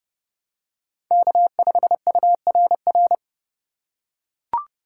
Morse code, heard as a keyed tone, about 30 wpm, 700 Hz. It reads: K5URR
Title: USA Call Signs - 5 Characters - 4x Character Spacing - 30wpm